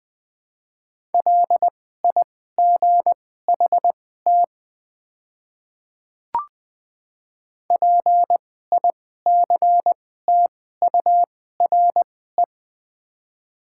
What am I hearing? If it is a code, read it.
Morse code, heard as a keyed tone, about 20 wpm, spaced as for 14 wpm, 700 Hz. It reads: LIGHT PICTURE
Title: 2x Character Spacing - Top 300 Words - 20wpm